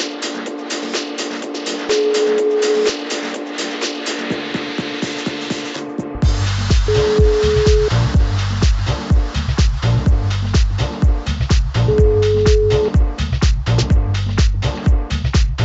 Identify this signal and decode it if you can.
Ringback tone (Europe, 425 Hz), with noise.